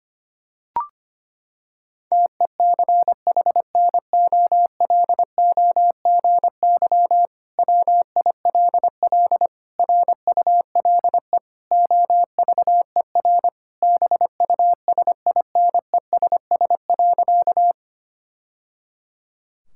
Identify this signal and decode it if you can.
Morse code, heard as a keyed tone, about 25 wpm, 700 Hz. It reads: TECHNOLOGY WILL RULE OVER BUSINESS.
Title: Sentences from Top 1000 Words - Review Part 5 - 25wpm